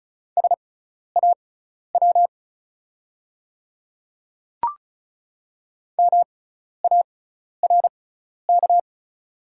Morse code, heard as a keyed tone, about 36 wpm, spaced as for 12 wpm, 690 Hz. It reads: SAW MARK
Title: 6x Character Spacing - Top 300 Words - 35wpm